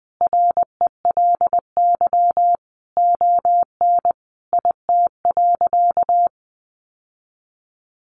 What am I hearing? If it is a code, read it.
Morse code, heard as a keyed tone, about 20 wpm, 700 Hz. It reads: RELY ON IT.
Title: Sentences from Top 1200 Words 20wpm